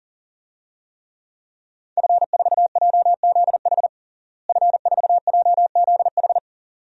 Morse code, heard as a keyed tone, about 40 wpm, 700 Hz. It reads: F4JZH F4JZH
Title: DXCC-100 Call Signs - 5 Characters - Repeated 2x - 40wpm